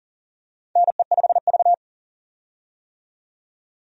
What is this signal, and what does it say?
Morse code, heard as a keyed tone, about 39 wpm, 700 Hz. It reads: NE5V
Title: USA Call Signs - 4 Characters - 40wpm